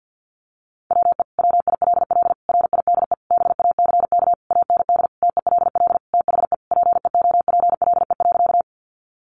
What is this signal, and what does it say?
Morse code, heard as a keyed tone, about 50 wpm, 700 Hz. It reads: WE WILL RIDE BACK AND TELL THE PEOPLE.
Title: Sentences from Top 1200 Words 50wpm